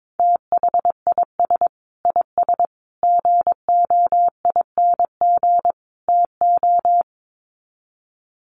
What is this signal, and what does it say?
Morse code, heard as a keyed tone, about 22 wpm, 700 Hz. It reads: THIS IS GOING TO
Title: Sets of 4 Words 22wpm